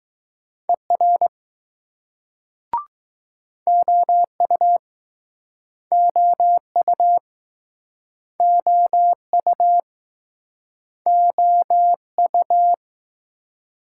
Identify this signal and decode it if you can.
Morse code, heard as a keyed tone, about 20 wpm, 700 Hz. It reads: ER OU OU OU OU